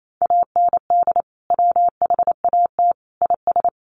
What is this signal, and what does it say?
Morse code, heard as a keyed tone, about 28 wpm, 700 Hz. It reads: AND WHAT IS
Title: Sentences from Top 100 Words 28wpm